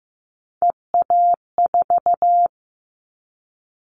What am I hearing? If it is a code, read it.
Morse code, heard as a keyed tone, about 15 wpm, 700 Hz. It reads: EA4
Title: Callsign Prefixes for Top 25 DXCC - 15wpm